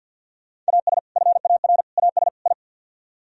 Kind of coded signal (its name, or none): Morse code